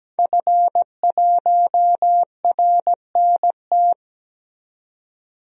Morse code, heard as a keyed tone, about 17 wpm, 700 Hz. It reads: F1RNT
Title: DXCC-100 Call Signs - 5 Characters - 17wpm